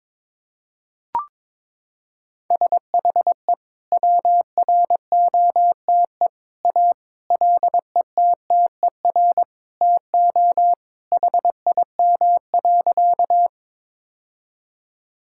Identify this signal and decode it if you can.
Morse code, heard as a keyed tone, about 22 wpm, 700 Hz. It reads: SHE WROTE A LETTER TO HIM.
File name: practice / 22wpm / Sentences with 2-Word Phrases - 22wpm